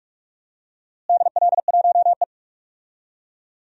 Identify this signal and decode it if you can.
Morse code, heard as a keyed tone, about 41 wpm, 700 Hz. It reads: DL1E
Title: DXCC-100 Call Signs - 4 Characters - Repeated 2x - 45wpm